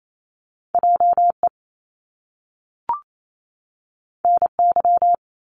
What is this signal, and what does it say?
Morse code, heard as a keyed tone, about 28 wpm, 700 Hz. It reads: JE NY
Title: Top Two Letter Combinations in Words 28wpm